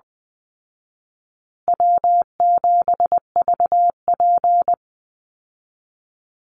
Morse code, heard as a keyed tone, about 20 wpm, 700 Hz. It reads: W7VP
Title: Call Signs Repeated 2x - 20wpm